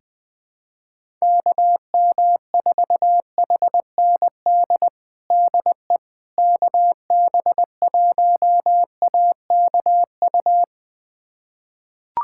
Morse code, heard as a keyed tone, about 20 wpm, 700 Hz. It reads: KM4HND DE KB1AKU
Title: USA Callsign DE Callsign 20wpm